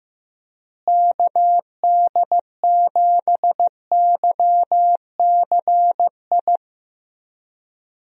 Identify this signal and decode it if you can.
Morse code, heard as a keyed tone, about 15 wpm, 700 Hz. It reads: KD7YCI
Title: USA Call Signs - 6 Characters - Repeated 2x - 15wpm